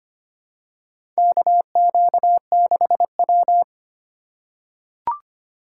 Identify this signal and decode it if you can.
Morse code, heard as a keyed tone, about 25 wpm, 700 Hz. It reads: KQ6W
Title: USA Call Signs - 3x Character Spacing - 25wpm